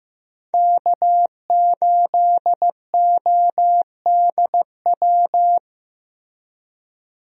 Morse code, heard as a keyed tone, about 15 wpm, 700 Hz. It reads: K8ODW
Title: USA Call Signs - 5 Characters - Repeated 3x - 15wpm